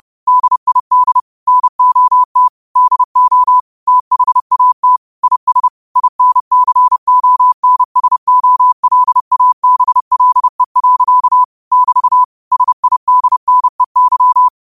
Morse code, heard as a keyed tone, about 30 wpm, 1000 Hz. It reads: DID NOT DO THAT IS INCONSOLABLE. <BT> SIDNEY